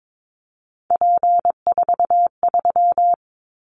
Morse code, heard as a keyed tone, about 22 wpm, 700 Hz. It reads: P43
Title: Callsign Prefixes for Top 100 DXCC - 22wpm